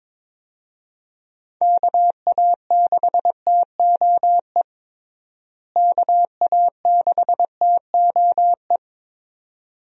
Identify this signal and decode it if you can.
Morse code, heard as a keyed tone, about 21 wpm, 700 Hz. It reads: KA6TOE KA6TOE
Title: USA Call Signs - 6 Characters - Repeated 2x - 22wpm